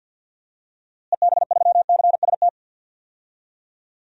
Morse code, heard as a keyed tone, about 52 wpm, 700 Hz. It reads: EB3CST
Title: DXCC-100 Call Signs - 6 Characters - Repeated 2x - 50wpm